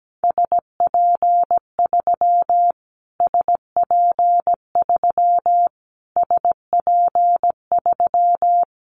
Morse code, heard as a keyed tone, about 17 wpm, 700 Hz. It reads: SP3 SP3 SP3
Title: Callsign Prefixes for Top 25 DXCC - Repeated 3x - 17wpm